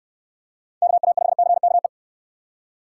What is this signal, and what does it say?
Morse code, heard as a keyed tone, about 64 wpm, 700 Hz. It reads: KA5CKE